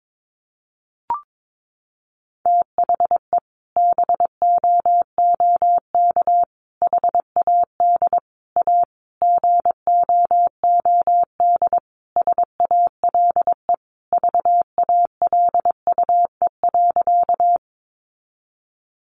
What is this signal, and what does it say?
Morse code, heard as a keyed tone, about 22 wpm, 700 Hz. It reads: THE BOOK HAD A GOOD SALE VALUE.
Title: Sentences from Top 1400 Words 22wpm